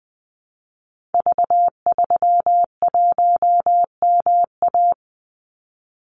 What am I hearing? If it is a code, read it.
Morse code, heard as a keyed tone, about 20 wpm, 700 Hz. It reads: V31MA
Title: Call Signs - 20wpm